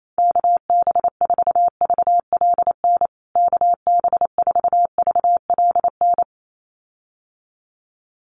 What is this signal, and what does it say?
Morse code, heard as a keyed tone, about 28 wpm, 700 Hz. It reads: KB4VLN KB4VLN
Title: USA Call Signs Repeated 2x 28wpm